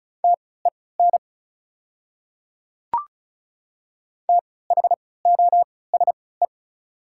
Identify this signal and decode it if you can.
Morse code, heard as a keyed tone, about 36 wpm, spaced as for 20 wpm, 690 Hz. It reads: TEN THOSE
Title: 3x Character Spacing - Top 500 Words - 35wpm